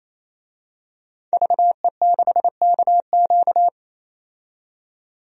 Morse code, heard as a keyed tone, about 28 wpm, 700 Hz. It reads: VE6KQ
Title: DXCC-100 Call Signs - 5 Characters - 28wpm